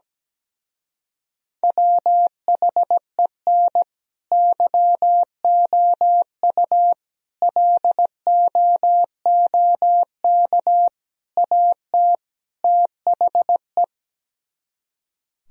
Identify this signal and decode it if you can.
Morse code, heard as a keyed tone, about 17 wpm, 700 Hz. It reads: WHEN YOU LOOK AT THE
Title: Speed-Racing - Sets of 5 Words 17wpm